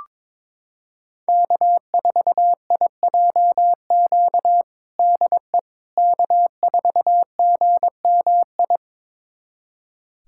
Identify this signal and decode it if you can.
Morse code, heard as a keyed tone, about 22 wpm, 700 Hz. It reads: K4IJQ DE K4GMI
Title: USA Callsign DE Callsign 22wpm